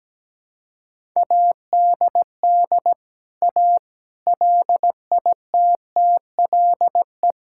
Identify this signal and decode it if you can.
Morse code, heard as a keyed tone, about 17 wpm, 700 Hz. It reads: ADD A LITTLE
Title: Sentences from Top 1000 Words - Review Part 5 - 17wpm